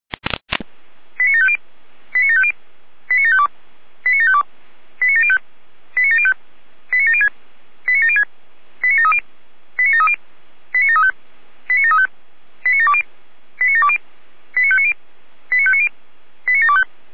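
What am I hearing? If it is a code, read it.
Five-tone selective call (ZVEI), heard as quick runs of five tones, tones of about 80 ms. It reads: 89750 89750 89752 89752 89705 89705 89706 89706 89730 89730 89735 89735 89720 89720 89590 89590 89725
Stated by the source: ZVEI five-tone call 89518 received by SDR, channel 455.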